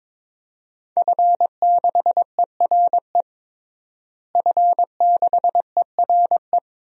Morse code, heard as a keyed tone, about 22 wpm, 700 Hz. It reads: F6ERE F6ERE
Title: DXCC-100 Call Signs - 5 Characters - Repeated 3x - 22wpm